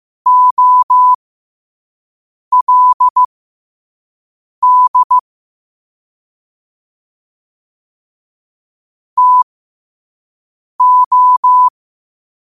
Morse code, heard as a keyed tone, about 15 wpm, spaced as for 5 wpm, 1000 Hz. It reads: OLD TO
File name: qotdfiles / 5wpm